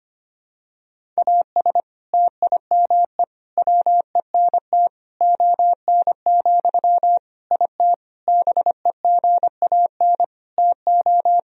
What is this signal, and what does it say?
Morse code, heard as a keyed tone, about 25 wpm, 700 Hz. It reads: AS TIME WENT ON, IT BEGAN TO
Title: Sentences from Top 1000 Words - Review Part 1 - 25wpm